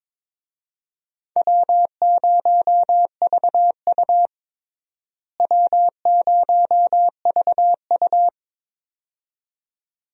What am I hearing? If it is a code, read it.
Morse code, heard as a keyed tone, about 21 wpm, 700 Hz. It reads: W0VU W0VU
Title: USA Call Signs - 4 Characters - Repeated 2x - 22wpm